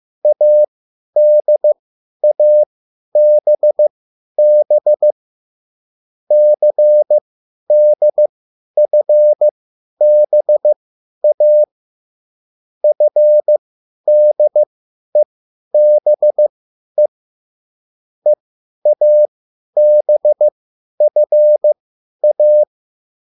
Morse code, heard as a keyed tone, about 15 wpm, spaced as for 10 wpm, 600 Hz. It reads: ADABB CDFBA FDEBE EABFA
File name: QSL / CW lesson